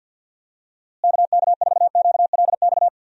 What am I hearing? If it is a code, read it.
Morse code, heard as a keyed tone, about 52 wpm, 700 Hz. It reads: KK4QLX
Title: USA Call Signs - 6 Characters - Repeated 3x - 50wpm